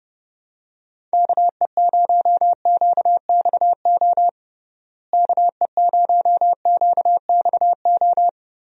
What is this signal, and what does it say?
Morse code, heard as a keyed tone, about 32 wpm, 700 Hz. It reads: KE0QXO KE0QXO
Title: USA Call Signs - 6 Characters - Repeated 3x - 30wpm